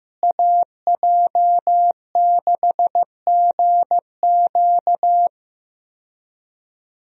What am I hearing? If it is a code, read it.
Morse code, heard as a keyed tone, about 15 wpm, 700 Hz. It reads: AJ6GQ
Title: USA Call Signs - 5 Characters - Repeated 3x - 15wpm